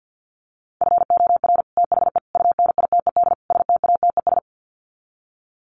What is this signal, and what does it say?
Morse code, heard as a keyed tone, about 50 wpm, 700 Hz. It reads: FOR THE UNITED STATES
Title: Sets of 4 Words 50wpm